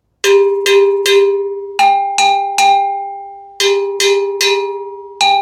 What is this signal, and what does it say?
There is no code.